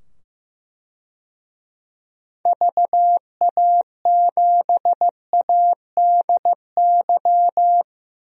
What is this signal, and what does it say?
Morse code, heard as a keyed tone, about 15 wpm, 700 Hz. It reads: VA7ADY
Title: DXCC-100 Call Signs - 6 Characters - 15wpm